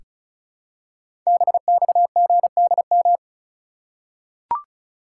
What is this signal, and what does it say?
Morse code, heard as a keyed tone, about 36 wpm, 690 Hz. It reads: BXGDM
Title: Code Groups - 5 Letters - 35wpm